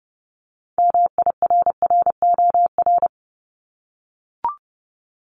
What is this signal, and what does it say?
Morse code, heard as a keyed tone, about 30 wpm, 700 Hz. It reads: MIRROR
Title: Multi-letter Phonemes 30wpm